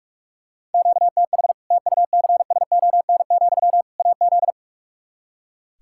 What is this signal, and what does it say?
Morse code, heard as a keyed tone, about 43 wpm, 700 Hz. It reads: QTH TUCSON, AZ
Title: Practice QSO - Exchange 1 - Beginning - 45wpm